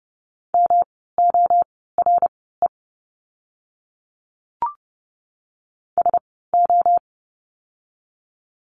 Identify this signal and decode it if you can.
Morse code, heard as a keyed tone, about 30 wpm, spaced as for 17 wpm, 700 Hz. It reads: MORE SO
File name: course / 30:10wpm / 115 Lesson 19f - Letter w - Words 30wpm